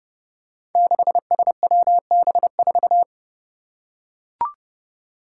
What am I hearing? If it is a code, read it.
Morse code, heard as a keyed tone, about 30 wpm, 700 Hz. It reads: 6SWB4
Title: Five Challenging Letter-Number Combinations 30wpm